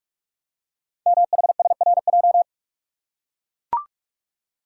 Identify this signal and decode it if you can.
Morse code, heard as a keyed tone, about 44 wpm, 690 Hz. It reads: MHSRJ
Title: Code Groups - 5 Letters - 45wpm